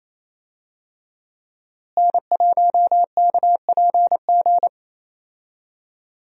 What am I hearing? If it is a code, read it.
Morse code, heard as a keyed tone, about 28 wpm, 700 Hz. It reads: N1KPG
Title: USA Call Signs - 5 Characters - Repeated 2x - 28wpm